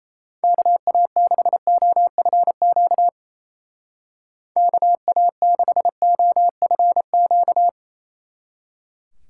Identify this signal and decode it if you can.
Morse code, heard as a keyed tone, about 32 wpm, 700 Hz. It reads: KA6OFQ KA6OFQ
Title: Speed Racing - USA 2x3 Call Signs 28wpm